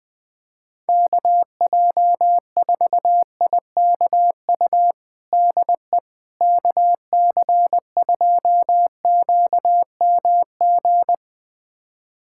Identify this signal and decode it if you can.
Morse code, heard as a keyed tone, about 21 wpm, 700 Hz. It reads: KJ4IKU DE KC2QMG
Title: USA Callsign DE Callsign - 6 Characters - 20wpm